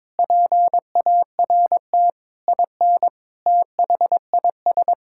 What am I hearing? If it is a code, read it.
Morse code, heard as a keyed tone, about 22 wpm, 700 Hz. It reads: PART IN THIS
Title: Sentences from Top 1000 Words - Review Part 5 - 22wpm